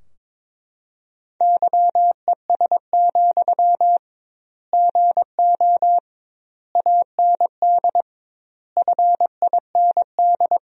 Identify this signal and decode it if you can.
Morse code, heard as a keyed tone, about 22 wpm, 700 Hz. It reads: YES, GO AND FIND
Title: 2x Word-Spacing - Sentences from Top 500 Words - 22wpm